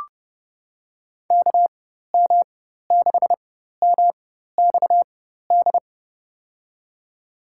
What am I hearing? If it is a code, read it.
Morse code, heard as a keyed tone, about 30 wpm, spaced as for 14 wpm, 700 Hz. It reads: KM6MXD
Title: USA Call Signs - 4x Character Spacing - 30wpm